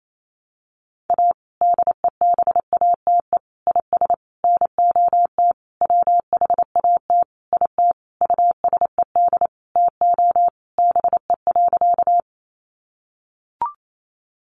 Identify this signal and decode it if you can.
Morse code, heard as a keyed tone, about 28 wpm, 700 Hz. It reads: A DEBATE IS NOT WHAT IT USED TO BE.
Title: Sentences from Top 1100 Words 28wpm